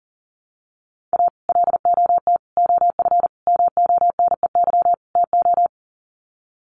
Morse code, heard as a keyed tone, about 40 wpm, 700 Hz. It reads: A LOT OF MONEY TO
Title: Sets of 5 Words Encore2 40wpm